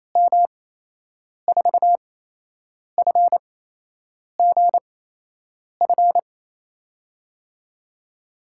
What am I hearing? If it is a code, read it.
Morse code, heard as a keyed tone, about 28 wpm, spaced as for 8 wpm, 700 Hz. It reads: M4FGF
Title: USA Call Signs - 6 Characters - 8x Character Spacing - 28wpm